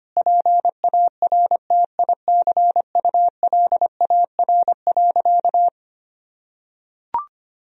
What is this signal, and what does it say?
Morse code, heard as a keyed tone, about 25 wpm, 700 Hz. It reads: PARTICULAR.
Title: Sentences from Top 1000 Words - Review Part 4 - 25wpm